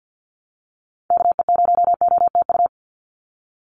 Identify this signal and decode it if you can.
Morse code, heard as a keyed tone, about 50 wpm, 700 Hz. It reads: KE0OTU
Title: USA Call Signs 50wpm